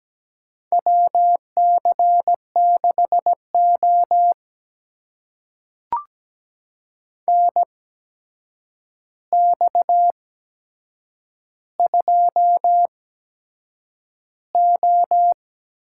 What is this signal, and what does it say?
Morse code, heard as a keyed tone, about 17 wpm, spaced as for 7 wpm, 700 Hz. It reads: WC6O NX2O
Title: USA Call Signs - 4 Characters - 8x Character Spacing - 17wpm